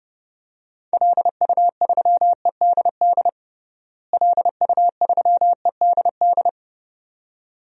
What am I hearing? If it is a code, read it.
Morse code, heard as a keyed tone, about 30 wpm, 700 Hz. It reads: LU3EDD LU3EDD
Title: DXCC-100 Call Signs - 6 Characters - Repeated 2x - 30wpm